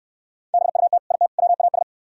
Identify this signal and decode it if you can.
Morse code, heard as a keyed tone, about 66 wpm, 700 Hz. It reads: BUT IT WAS